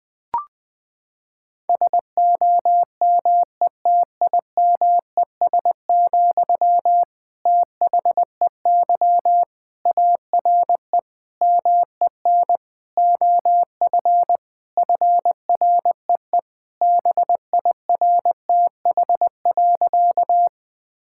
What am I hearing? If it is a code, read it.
Morse code, heard as a keyed tone, about 20 wpm, 700 Hz. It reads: SOMETIMES, THEY ARE MEN OF FREE BIRTH.
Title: Sentences from Top 1000 Words - Review Part 5 - 20wpm